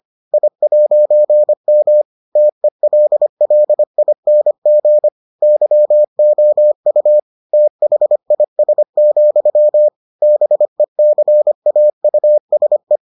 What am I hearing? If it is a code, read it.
Morse code, heard as a keyed tone, about 25 wpm, 600 Hz. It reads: I'M TELLING YOU THIS, BECAUSE